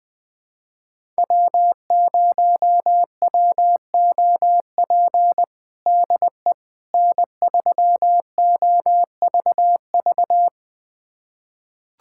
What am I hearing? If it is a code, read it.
Morse code, heard as a keyed tone, about 19 wpm, 700 Hz. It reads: W0WOP DE N3OVV
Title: USA Callsign DE Callsign - 5 Characters - 20wpm